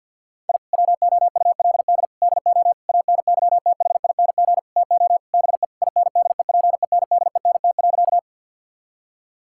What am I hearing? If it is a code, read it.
Morse code, heard as a keyed tone, about 50 wpm, 700 Hz. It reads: I WOULD DO ANYTHING TO BE INDEPENDENT.
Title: Sentences from Top 1000 Words - Review Part 6 - 50wpm